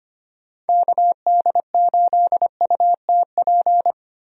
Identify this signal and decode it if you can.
Morse code, heard as a keyed tone, about 25 wpm, 700 Hz. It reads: KD8UTP
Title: USA Call Signs - 6 Characters - Repeated 2x - 25wpm